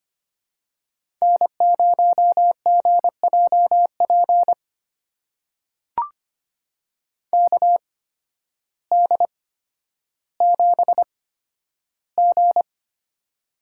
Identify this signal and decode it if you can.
Morse code, heard as a keyed tone, about 25 wpm, spaced as for 12 wpm, 700 Hz. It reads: N0GJP KD7G
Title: USA Call Signs - 8x Character Spacing - 25wpm